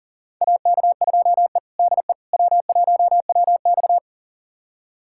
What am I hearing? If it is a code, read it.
Morse code, heard as a keyed tone, about 40 wpm, 700 Hz. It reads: AK2E DE W1WX
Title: USA Callsign DE Callsign - 4 Characters - 40wpm